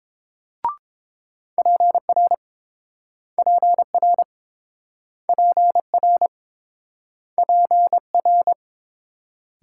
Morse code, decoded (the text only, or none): PR PR PR PR